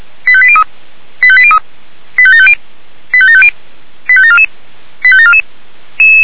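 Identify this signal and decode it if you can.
Five-tone selective call (ZVEI), heard as quick runs of five tones, tones of about 70 ms. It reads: 85803 85803 85750 85750 85744 85744
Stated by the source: ZVEI five-tone selective call 85730, received via SDR.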